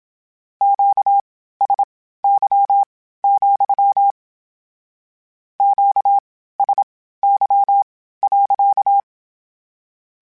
Morse code, heard as a keyed tone, about 27 wpm, spaced as for 15 wpm, 800 Hz. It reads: QSY, QSY.